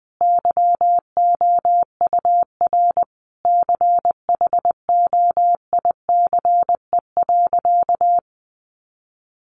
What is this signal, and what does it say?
Morse code, heard as a keyed tone, about 20 wpm, 700 Hz. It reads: YOUR CHOICE.
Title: Sentences from Top 900 Words 20wpm